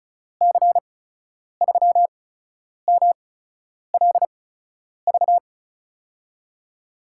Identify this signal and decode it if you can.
Morse code, heard as a keyed tone, about 36 wpm, spaced as for 10 wpm, 700 Hz. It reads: C3MLV
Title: USA Call Signs - 6 Characters - 8x Character Spacing - 35wpm